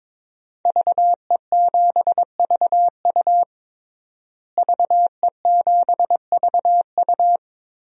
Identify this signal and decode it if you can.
Morse code, heard as a keyed tone, about 22 wpm, 700 Hz. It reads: VE7VU VE7VU
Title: DXCC-100 Call Signs - 5 Characters - Repeated 3x - 22wpm